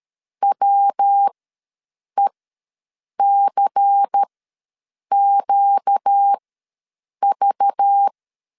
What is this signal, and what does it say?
Morse code, heard as a keyed tone, about 13 wpm, spaced as for 7 wpm, 800 Hz. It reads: WECQV
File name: LEZIONI / 5 LBVC / 5 LBVC ws 12wpm 3159